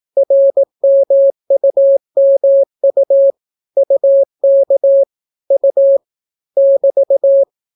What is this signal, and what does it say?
Morse code, heard as a keyed tone, about 18 wpm, 550 Hz. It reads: RMUMU UK U <BT>